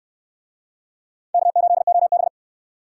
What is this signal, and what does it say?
Morse code, heard as a keyed tone, about 67 wpm, 700 Hz. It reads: K8OB